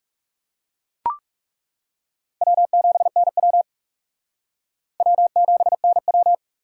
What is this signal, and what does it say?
Morse code, heard as a keyed tone, about 37 wpm, 700 Hz. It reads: W7NW W7NW
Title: Speed Racing - USA 1x2 Call Signs 30wpm